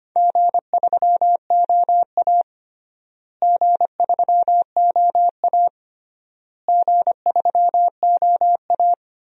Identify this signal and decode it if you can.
Morse code, heard as a keyed tone, about 25 wpm, 700 Hz. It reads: G3OA G3OA G3OA
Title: DXCC-100 Call Signs - 4 Characters - Repeated 3x - 25wpm